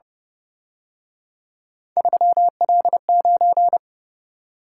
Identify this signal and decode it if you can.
Morse code, heard as a keyed tone, about 29 wpm, 700 Hz. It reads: 3L9